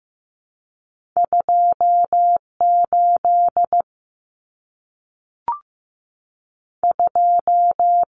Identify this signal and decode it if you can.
Morse code, heard as a keyed tone, about 15 wpm, 700 Hz. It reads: 28 2
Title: Common Cut Numbers - 2-Digit 15wpm